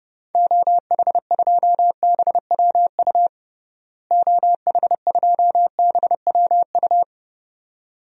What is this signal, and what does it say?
Morse code, heard as a keyed tone, about 30 wpm, 700 Hz. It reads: OH2BWU OH2BWU
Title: DXCC-100 Call Signs - 6 Characters - Repeated 2x - 30wpm